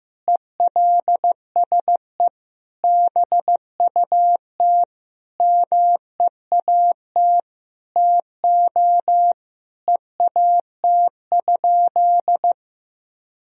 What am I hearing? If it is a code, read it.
Morse code, heard as a keyed tone, about 16 wpm, 700 Hz. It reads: ELSE BUT MEAT TO EAT?
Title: Sentences from Top 1000 Words - Review Part 3 - 15wpm